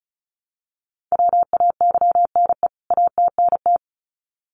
Morse code, heard as a keyed tone, about 34 wpm, 700 Hz. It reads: WAYNE ATNT
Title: CWT Exchange 35wpm